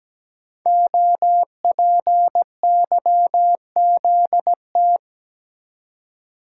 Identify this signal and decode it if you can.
Morse code, heard as a keyed tone, about 17 wpm, 700 Hz. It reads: OPYZT
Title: Code Groups - 5 Letters - 17wpm